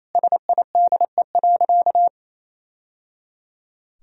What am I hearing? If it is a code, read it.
Morse code, heard as a keyed tone, about 28 wpm, 700 Hz. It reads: SIDE.
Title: Sentences from Top 1000 Words - Review Part 1 - 28wpm